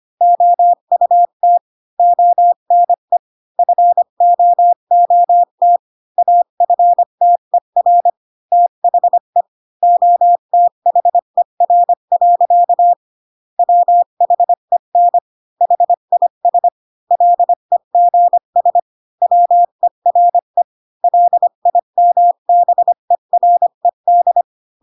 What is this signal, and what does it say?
Morse code, heard as a keyed tone, about 25 wpm, 700 Hz. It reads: OUT ONE FOOT AFTER THE OTHER. WHEN HIS LEGS WERE LIMBERED
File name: Pinocchio / Pinocchio ch3 20 25 ik1hge